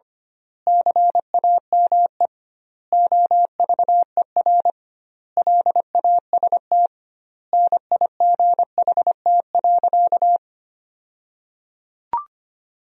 Morse code, heard as a keyed tone, about 25 wpm, 700 Hz. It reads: CAME OVER LAST NIGHT.
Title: 2x Word-Spacing - Sentences from Top 300 Words - 25wpm